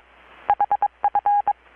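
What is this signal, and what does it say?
Morse code, heard as a keyed tone, about 22 wpm, 820 Hz. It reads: HF
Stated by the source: Morse from KPH's Night of Nights XX maritime radiotelegraphy event, received in Chiapas, Mexico about 3,800 km away on a Kenwood R-600 with a grounded 10 m long-wire antenna.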